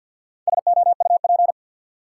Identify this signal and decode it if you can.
Morse code, heard as a keyed tone, about 48 wpm, 700 Hz. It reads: SOUP
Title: Sentences from Top 1000 Words - Review Part 6 - New Word Drill - 50wpm